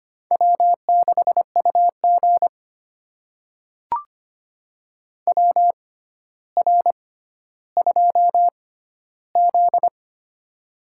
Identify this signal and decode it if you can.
Morse code, heard as a keyed tone, about 25 wpm, spaced as for 13 wpm, 700 Hz. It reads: W6UG WR2Z
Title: USA Call Signs - 4 Characters - 6x Character Spacing - 25wpm